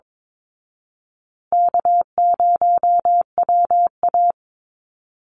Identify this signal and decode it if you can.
Morse code, heard as a keyed tone, about 22 wpm, 700 Hz. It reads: K0WA
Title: Easy Call Signs Repeated 3x - 22wpm